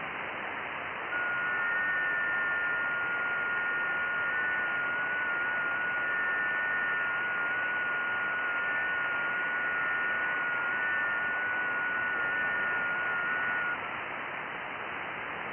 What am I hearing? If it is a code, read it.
FT8.
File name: IC-7100